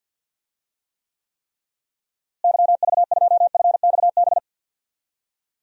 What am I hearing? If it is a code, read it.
Morse code, heard as a keyed tone, about 50 wpm, 700 Hz. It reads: YV2FXB
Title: DXCC-100 Call Signs - 6 Characters - Repeated 3x - 50wpm